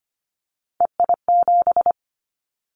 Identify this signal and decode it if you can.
Morse code, heard as a keyed tone, about 25 wpm, 690 Hz. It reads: EI7
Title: Callsign Prefixes for Top 50 DXCC - 25wpm